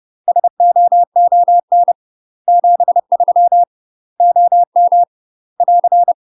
Morse code, heard as a keyed tone, about 30 wpm, 700 Hz. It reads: SOON 73 OM <AR>
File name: UK MASTER QSOs TRAINING / 30WPM / UK MASTER QSOs 0049